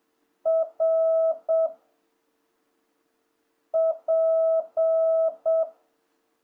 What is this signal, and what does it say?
Morse code, heard as a keyed tone, about 7 wpm, 660 Hz. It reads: R P
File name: jh8cbh / 2026cw